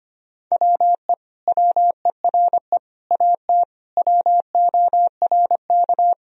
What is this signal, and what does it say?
Morse code, heard as a keyed tone, about 25 wpm, 700 Hz. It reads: WE WERE AT WORK
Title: Sentences from Top 1000 Words - Review Part 1 - 25wpm